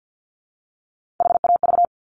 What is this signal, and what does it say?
Morse code, heard as a keyed tone, about 50 wpm, 700 Hz. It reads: HAV
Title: Top Three Letter Combinations in Words 50wpm